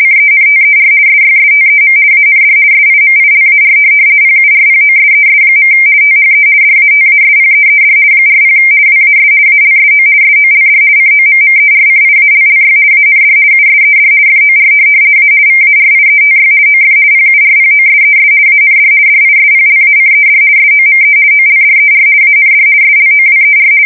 RTTY, about 50 baud. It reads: CENTER SYSTEM AND SIMILAR PURPOSES. THE DATASPEED 40 WAS ALSO SOLD COMMERCIALLY AND USED FOR A VARIETY OF PURPOSES AND DISPLACED SOME OLDER AND SLOWER
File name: fsk170-50B / teletype-corp